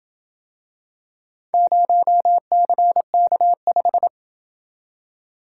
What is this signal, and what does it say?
Morse code, heard as a keyed tone, about 27 wpm, 700 Hz. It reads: 0CK5